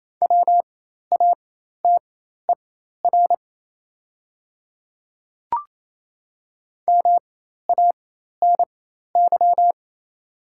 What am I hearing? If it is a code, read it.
Morse code, heard as a keyed tone, about 27 wpm, spaced as for 13 wpm, 700 Hz. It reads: WATER MANY
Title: 4x Character Spacing - Top 100 Words - 28wpm